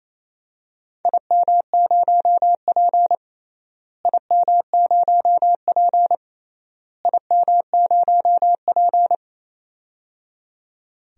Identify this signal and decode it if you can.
Morse code, heard as a keyed tone, about 28 wpm, 700 Hz. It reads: IM0P IM0P IM0P
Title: DXCC-100 Call Signs - 4 Characters - Repeated 3x - 28wpm